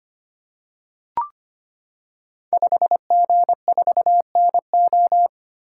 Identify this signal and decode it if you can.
Morse code, heard as a keyed tone, about 25 wpm, 700 Hz. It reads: 5G4NO
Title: Code Groups - 5 Letter:Numbers - 25wpm